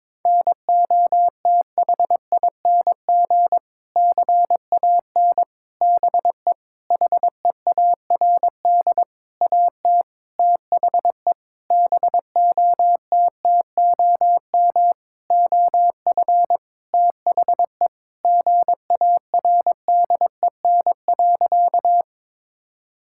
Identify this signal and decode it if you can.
Morse code, heard as a keyed tone, about 22 wpm, 700 Hz. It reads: NOTHING CAN BE HEARD AT THE BOTTOM OF THE GARDEN.
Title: Sentences from Top 1000 Words - Review Part 6 - 22wpm